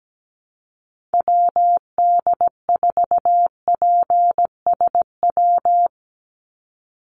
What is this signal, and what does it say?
Morse code, heard as a keyed tone, about 17 wpm, 700 Hz. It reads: WD4PSW
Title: USA Call Signs 17wpm